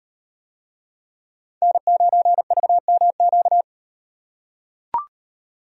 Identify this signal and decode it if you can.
Morse code, heard as a keyed tone, about 38 wpm, 700 Hz. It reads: N9VMQ